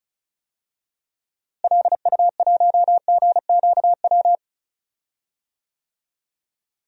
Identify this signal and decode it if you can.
Morse code, heard as a keyed tone, about 34 wpm, 700 Hz. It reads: LU1GQW
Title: DXCC-100 Call Signs - 6 Characters - Repeated 2x - 35wpm